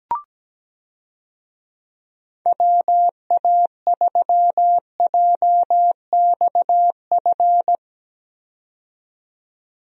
Morse code, heard as a keyed tone, about 17 wpm, 700 Hz. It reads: WA3JXF
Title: USA Call Signs - 6 Characters - 17wpm